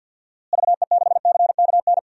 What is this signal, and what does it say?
Morse code, heard as a keyed tone, about 51 wpm, 700 Hz. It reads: VE6CKN
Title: DXCC-100 Call Signs - 6 Characters - Repeated 3x - 50wpm